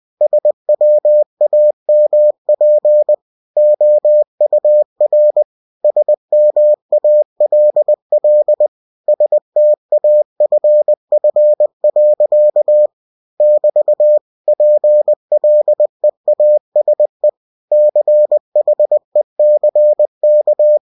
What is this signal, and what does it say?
Morse code, heard as a keyed tone, about 20 wpm, 600 Hz. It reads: SWAMP OUR SMALL STAFF. <BT> PLEASE CHECK
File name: cw-books / winnie-the-pooh / winnie-20wpm-page-0122